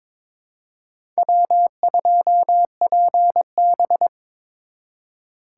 Morse code, heard as a keyed tone, about 26 wpm, 700 Hz. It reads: W2PB
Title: Speed Racing - USA 1x2 Call Signs 22wpm